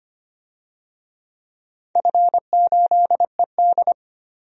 Morse code, heard as a keyed tone, about 25 wpm, 700 Hz. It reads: F8ED
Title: DXCC-100 Call Signs - 4 Characters - 25wpm